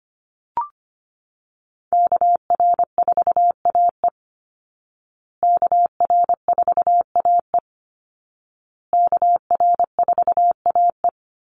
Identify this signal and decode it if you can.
Morse code, heard as a keyed tone, about 25 wpm, 700 Hz. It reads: KR4AE KR4AE KR4AE
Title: Call Signs Repeated 2x - 25wpm